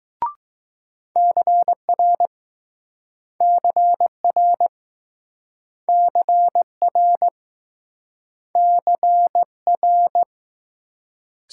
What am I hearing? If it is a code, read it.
Morse code, heard as a keyed tone, about 21 wpm, 700 Hz. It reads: CR CR CR CR